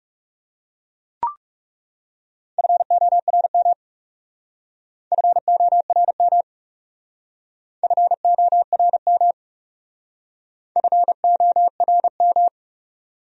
Morse code, heard as a keyed tone, about 36 wpm, 700 Hz. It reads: FORM FORM FORM FORM
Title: Speed-Racing Top 200 Words 30wpm